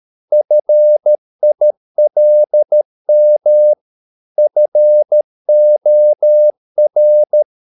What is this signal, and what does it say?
Morse code, heard as a keyed tone, about 13 wpm, 600 Hz. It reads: FILM FOR
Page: Ferie 13 wpm